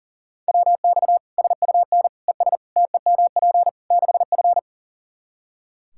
Morse code, heard as a keyed tone, about 40 wpm, 700 Hz. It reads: WX SUN ES TEMP 6F
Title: Practice QSO - Exchange 2 - Info - 40wpm